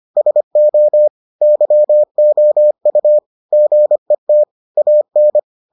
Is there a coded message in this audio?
Yes: Morse code from 0.2 to 5.4 s, heard as a keyed tone, about 25 wpm, 600 Hz. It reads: SO YOU GET AN